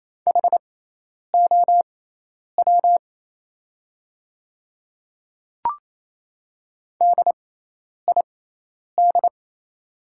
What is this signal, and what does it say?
Morse code, heard as a keyed tone, about 28 wpm, spaced as for 10 wpm, 700 Hz. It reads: HOW DID
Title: 6x Character Spacing - Top 100 Words - 28wpm